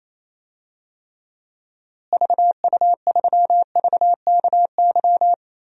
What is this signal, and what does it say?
Morse code, heard as a keyed tone, about 27 wpm, 700 Hz. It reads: VU3VKY
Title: DXCC-100 Call Signs - 6 Characters - Repeated 3x - 28wpm